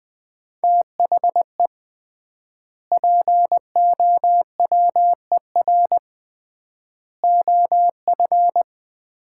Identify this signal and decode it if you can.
Morse code, heard as a keyed tone, about 20 wpm, 700 Hz. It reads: THE POWER OF
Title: 3x Word-Spacing - Sets of 4 Words - 20wpm